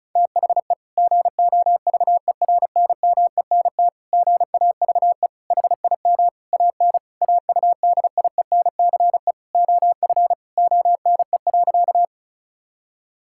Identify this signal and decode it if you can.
Morse code, heard as a keyed tone, about 36 wpm, 690 Hz. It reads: THE GOVERNMENT GAVE HIM AN AUDIENCE OF ONE.
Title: Sentences from Top 1000 Words - Review Part 5 - 35wpm